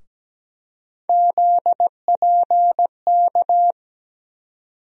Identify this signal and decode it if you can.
Morse code, heard as a keyed tone, about 17 wpm, 700 Hz. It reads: ZPK
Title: USA Call Sign Postfixes - 3 Letters 17wpm